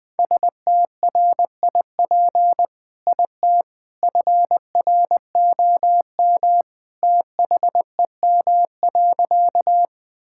Morse code, heard as a keyed tone, about 20 wpm, 700 Hz. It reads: STRIP IT FROM THEM.
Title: Sentences from Top 1600 Words 20wpm